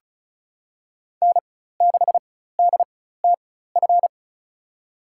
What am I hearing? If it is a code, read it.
Morse code, heard as a keyed tone, about 36 wpm, spaced as for 16 wpm, 700 Hz. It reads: N6DTF
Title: USA Call Signs - 4x Character Spacing - 35wpm